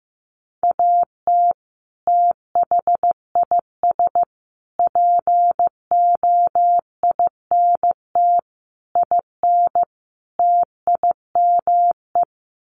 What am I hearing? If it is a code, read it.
Morse code, heard as a keyed tone, about 15 wpm, 700 Hz. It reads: AT THIS POINT IN TIME
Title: Sets of 5 Words Encore 15wpm